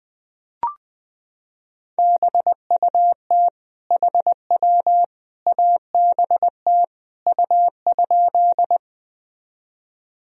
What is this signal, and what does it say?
Morse code, heard as a keyed tone, about 21 wpm, 700 Hz. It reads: BUT HW ABT U?
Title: Sentences from Top 100 Words and Abbreviations 20wpm